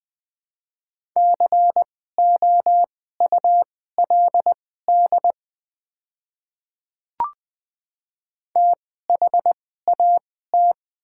Morse code, heard as a keyed tone, about 20 wpm, spaced as for 14 wpm, 700 Hz. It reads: COULD THAT
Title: 2x Character Spacing - Top 100 Words - 20wpm